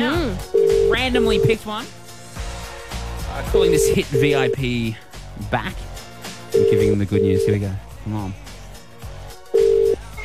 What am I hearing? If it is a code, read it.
Ringback tone (UK), 400 + 450 Hz, with speech.